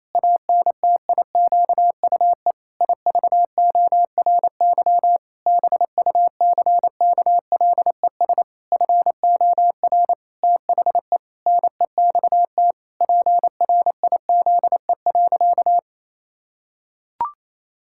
Morse code, heard as a keyed tone, about 28 wpm, 700 Hz. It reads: ANTIQUE IVORY BUCKLES FOR THE NEXT PRIZE.
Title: Pangram Challenge v2 - 28wpm